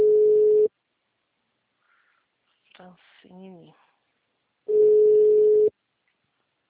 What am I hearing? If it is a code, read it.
Ringback tone (Europe, 425 Hz), with speech.